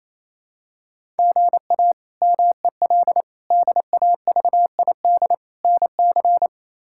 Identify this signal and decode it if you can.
Morse code, heard as a keyed tone, about 28 wpm, 700 Hz. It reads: GA MEL DAVID NC
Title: K1USN SST QSO Practice - 28wpm